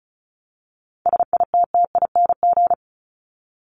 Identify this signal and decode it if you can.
Morse code, heard as a keyed tone, about 34 wpm, 700 Hz. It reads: SITTING
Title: ING Suffix - Top 100 Words with Base Words 35wpm